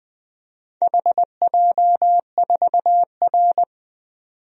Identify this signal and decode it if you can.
Morse code, heard as a keyed tone, about 20 wpm, 700 Hz. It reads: HJ4R
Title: DXCC-100 Call Signs - 4 Characters - 20wpm